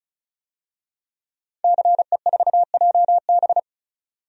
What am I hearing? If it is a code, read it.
Morse code, heard as a keyed tone, about 36 wpm, 700 Hz. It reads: CE4JB
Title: DXCC-100 Call Signs - 5 Characters - 35wpm